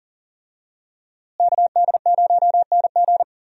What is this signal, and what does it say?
Morse code, heard as a keyed tone, about 40 wpm, 700 Hz. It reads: KD0NG
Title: USA Call Signs - 5 Characters - Repeated 3x - 40wpm